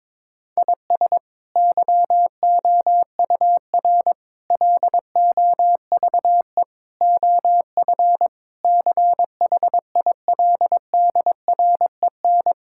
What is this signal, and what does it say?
Morse code, heard as a keyed tone, about 22 wpm, 700 Hz. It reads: IS YOUR LOVE OF CHILDREN
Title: Sentences from Top 1000 Words - Review Part 5 - 22wpm